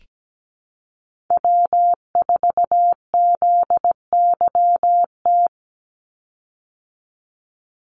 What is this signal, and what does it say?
Morse code, heard as a keyed tone, about 17 wpm, 700 Hz. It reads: W4ZYT
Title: Call Signs Repeated 3x - 17wpm